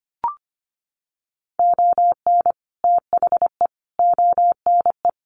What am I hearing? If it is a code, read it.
Morse code, heard as a keyed tone, about 25 wpm, 700 Hz. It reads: ON THE ONE
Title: Sets of 4 Words 25wpm